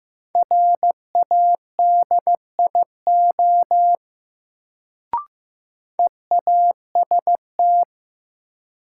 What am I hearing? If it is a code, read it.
Morse code, heard as a keyed tone, about 16 wpm, 700 Hz. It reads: RADIO EAST
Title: Sentences from Top 1000 Words - Review Part 3 - New Word Drill - 15wpm